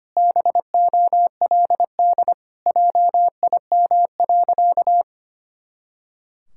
Morse code, heard as a keyed tone, about 25 wpm, 700 Hz. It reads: BOLD JIM.
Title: Pangram Challenge v2 - 25wpm